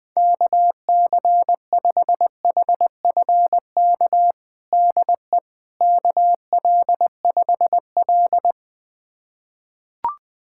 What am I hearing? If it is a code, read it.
Morse code, heard as a keyed tone, about 20 wpm, 700 Hz. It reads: KC5HFK DE KL5L
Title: USA Callsign DE Callsign 20wpm